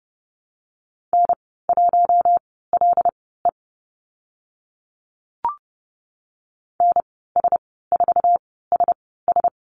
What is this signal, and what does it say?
Morse code, heard as a keyed tone, about 30 wpm, spaced as for 17 wpm, 700 Hz. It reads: N1LE NS4SS
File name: Lesson 15g - Number 5 - Call Signs 30wpm